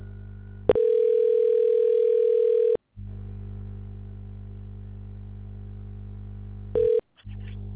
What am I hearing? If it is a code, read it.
Ringback tone (North America), 440 + 480 Hz.